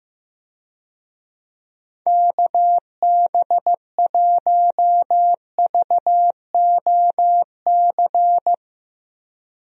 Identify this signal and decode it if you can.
Morse code, heard as a keyed tone, about 15 wpm, 700 Hz. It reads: KB1VOC
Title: USA Call Signs - 6 Characters - Repeated 2x - 15wpm